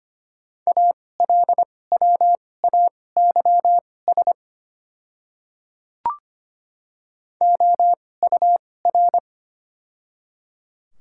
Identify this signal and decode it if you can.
Morse code, heard as a keyed tone, about 25 wpm, spaced as for 18 wpm, 700 Hz. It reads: ALWAYS OUR
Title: 2x Character Spacing - Top 300 Words - 25wpm